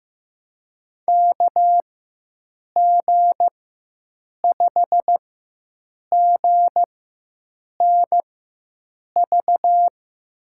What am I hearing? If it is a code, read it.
Morse code, heard as a keyed tone, about 15 wpm, spaced as for 7 wpm, 700 Hz. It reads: KG5GNV
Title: USA Call Signs - 6 Characters - 4x Character Spacing - 15wpm